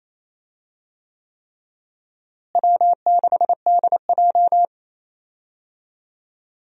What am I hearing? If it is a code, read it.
Morse code, heard as a keyed tone, about 28 wpm, 700 Hz. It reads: W6DJ
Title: USA Call Signs - 4 Characters - Repeated 3x - 28wpm